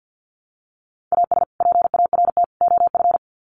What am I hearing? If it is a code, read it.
Morse code, heard as a keyed tone, about 51 wpm, 700 Hz. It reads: AS PART OF
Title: Sets of 3 Words 50wpm